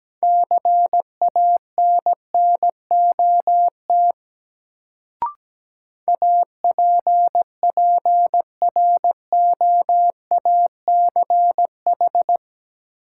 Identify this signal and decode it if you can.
Morse code, heard as a keyed tone, about 17 wpm, 700 Hz. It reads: CANNOT APPROACH
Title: Sentences from Top 1000 Words - Review Part 6 - New Word Drill - 17wpm